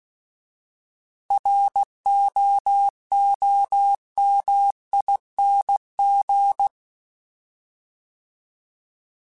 Morse code, heard as a keyed tone, about 16 wpm, 800 Hz. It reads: ROOMING
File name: Alphabetical / Alphabetical mp3s / Rs @15